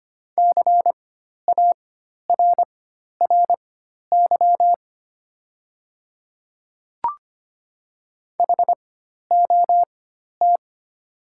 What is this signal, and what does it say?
Morse code, heard as a keyed tone, about 25 wpm, spaced as for 12 wpm, 700 Hz. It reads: CARRY HOT
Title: 4x Character Spacing - Top 500 Words - 25wpm